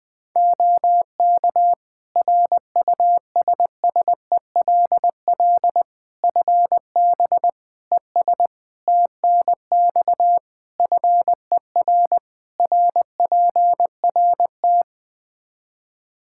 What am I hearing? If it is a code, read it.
Morse code, heard as a keyed tone, about 20 wpm, 700 Hz. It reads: OK RUSSELL FB ES TNX FER RPRT
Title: Practice QSO - Exchange 2 - Info - 20wpm